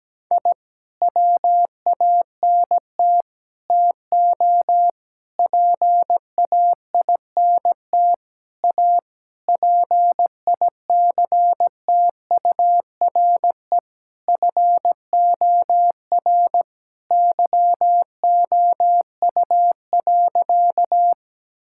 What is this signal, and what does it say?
Morse code, heard as a keyed tone, about 17 wpm, 700 Hz. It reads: I WANT TO PAINT A PICTURE FOR YOU.
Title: Sentences from Top 1000 Words - Review Part 6 - 17wpm